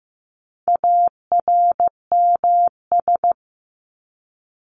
Morse code, heard as a keyed tone, about 15 wpm, 700 Hz. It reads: ARMS
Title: ES:S Suffix - Top 100 Words with Base Words 15wpm